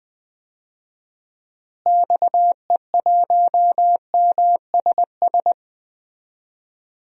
Morse code, heard as a keyed tone, about 19 wpm, 700 Hz. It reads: XE1MSS
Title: DXCC-100 Call Signs - 6 Characters - Repeated 3x - 20wpm